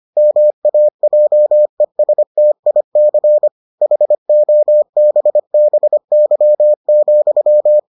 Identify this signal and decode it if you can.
Morse code, heard as a keyed tone, about 25 wpm, 600 Hz. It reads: MAJESTIC HOBBY,